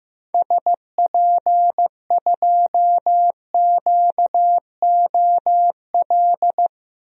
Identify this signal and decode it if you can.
Morse code, heard as a keyed tone, about 15 wpm, 700 Hz. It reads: SP2QOL
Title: DXCC-100 Call Signs - 6 Characters - 15wpm